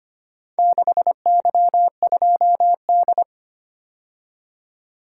Morse code, heard as a keyed tone, about 25 wpm, 700 Hz. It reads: 6Y2D
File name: practice / 25wpm / Four Letter-Number Combinations - 25wpm